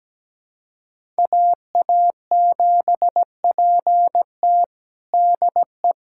Morse code, heard as a keyed tone, about 17 wpm, 700 Hz. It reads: AA7PT DE
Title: USA Callsign DE Callsign - 5 Characters - 17wpm